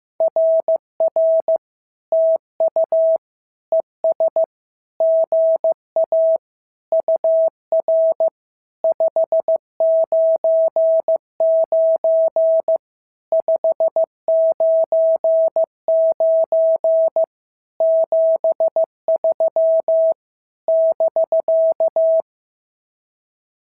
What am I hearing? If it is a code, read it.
Morse code, heard as a keyed tone, about 15 wpm, 650 Hz. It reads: RR TU ES GA UR 599 599 73 <BK>
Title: SOTA QSO Practice 15wpm